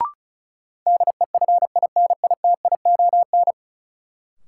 Morse code, heard as a keyed tone, about 36 wpm, 700 Hz. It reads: DEFINITION
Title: Sentences from Top 1000 Words - Review Part 4 - New Word Drill - 35wpm